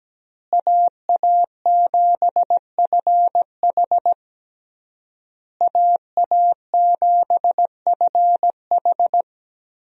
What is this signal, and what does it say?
Morse code, heard as a keyed tone, about 17 wpm, 700 Hz. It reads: AA7FH AA7FH
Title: USA Call Signs - 5 Characters - Repeated 3x - 17wpm